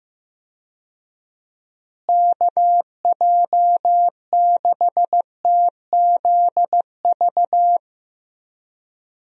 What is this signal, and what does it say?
Morse code, heard as a keyed tone, about 15 wpm, 700 Hz. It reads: KJ6TZV
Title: USA Call Signs - 6 Characters - Repeated 2x - 15wpm